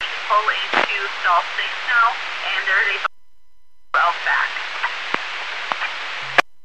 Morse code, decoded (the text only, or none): none